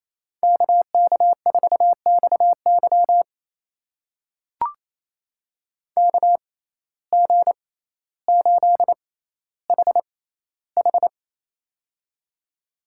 Morse code, heard as a keyed tone, about 27 wpm, spaced as for 14 wpm, 700 Hz. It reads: KK4XY KG8HH
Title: USA Call Signs - 5 Characters - 6x Character Spacing - 28wpm